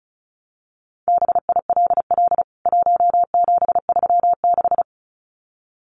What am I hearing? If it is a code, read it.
Morse code, heard as a keyed tone, about 34 wpm, 700 Hz. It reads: BILL 1736
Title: CWT Exchange 35wpm